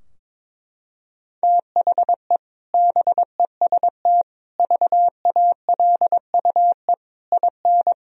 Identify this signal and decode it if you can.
Morse code, heard as a keyed tone, about 22 wpm, 700 Hz. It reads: THE BEST VALUE IN
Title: Sentences from Top 1000 Words - Review Part 3 - 22wpm